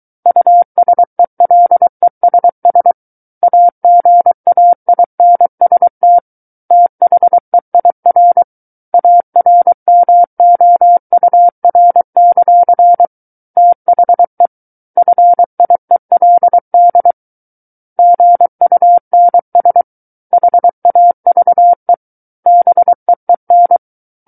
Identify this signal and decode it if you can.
Morse code, heard as a keyed tone, about 23 wpm, 700 Hz. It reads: USELESS AGAINST THEIR ARMOUR; THE FIELD GUNS HAVE BEEN
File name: Book One / 14-Chapter 14 - 23 WPM